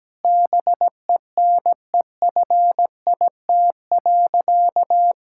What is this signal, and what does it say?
Morse code, heard as a keyed tone, about 17 wpm, 700 Hz. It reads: BENEFIT.
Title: Sentences from Top 1000 Words - Review Part 6 - 17wpm